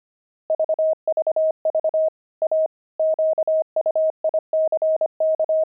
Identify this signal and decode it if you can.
Morse code, heard as a keyed tone, about 25 wpm, 630 Hz. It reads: VVV A QUICK
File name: pangram-fs-25-cs-25-enemy